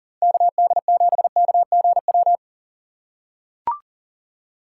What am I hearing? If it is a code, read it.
Morse code, heard as a keyed tone, about 40 wpm, 700 Hz. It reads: KD7KGW
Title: USA Call Signs - 4x Character Spacing - 40wpm